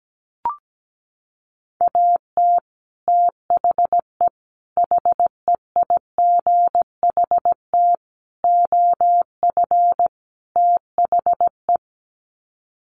Morse code, heard as a keyed tone, about 17 wpm, 700 Hz. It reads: AT THE HEIGHT OF THE
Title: Sets of 5 Words 17wpm